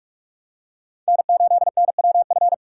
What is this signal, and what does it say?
Morse code, heard as a keyed tone, about 46 wpm, 700 Hz. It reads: N8NWF